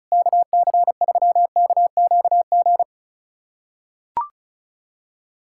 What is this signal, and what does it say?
Morse code, heard as a keyed tone, about 34 wpm, 700 Hz. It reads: KC3KQG
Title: USA Callsign DE Callsign - 6 Characters - 35wpm